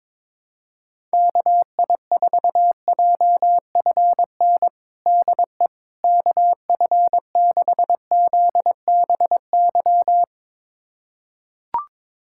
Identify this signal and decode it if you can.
Morse code, heard as a keyed tone, about 21 wpm, 700 Hz. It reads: KI4JFN DE KF6ZBY
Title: USA Callsign DE Callsign - 6 Characters - 22wpm